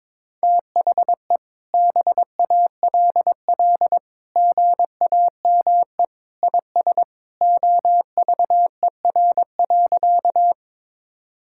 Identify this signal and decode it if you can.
Morse code, heard as a keyed tone, about 22 wpm, 700 Hz. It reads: THE BALL GAME IS OVER.
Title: Sentences from Top 1000 Words - Review Part 6 - 22wpm